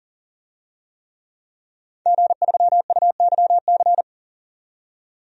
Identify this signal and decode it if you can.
Morse code, heard as a keyed tone, about 40 wpm, 700 Hz. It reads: G3UYC
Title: DXCC-100 Call Signs - 5 Characters - 40wpm